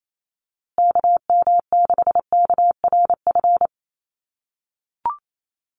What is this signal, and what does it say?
Morse code, heard as a keyed tone, about 28 wpm, 700 Hz. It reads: KM6KRF
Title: USA Call Signs Repeated 3x 28wpm